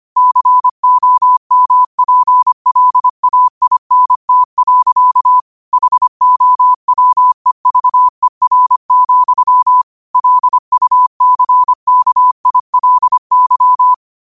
Morse code, heard as a keyed tone, about 25 wpm, 1000 Hz. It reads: COMPLAINT. HOWEVER, LUCKILY